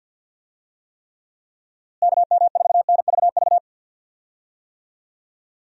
Morse code, heard as a keyed tone, about 53 wpm, 700 Hz. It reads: KM4NVV
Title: USA Call Signs - 6 Characters - Repeated 2x - 50wpm